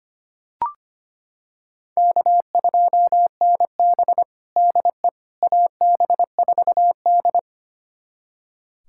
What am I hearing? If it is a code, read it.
Morse code, heard as a keyed tone, about 25 wpm, 700 Hz. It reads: K2NB DE AB4D
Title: USA Callsign DE Callsign - 4 Characters - 25wpm